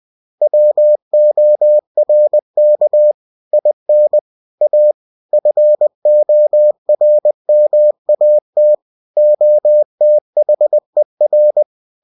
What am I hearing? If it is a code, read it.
Morse code, heard as a keyed tone, about 20 wpm, 600 Hz. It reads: WORK IN A FORMAT OTHER
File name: cw-books / winnie-the-pooh / winnie-20wpm-page-0116